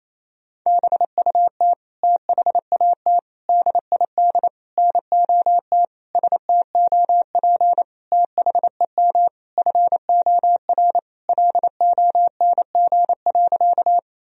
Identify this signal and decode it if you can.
Morse code, heard as a keyed tone, about 28 wpm, 700 Hz. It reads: BUT THAT DID NOT STOP THEM FOR LONG.
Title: Sentences from Top 1000 Words - Review Part 1 - 28wpm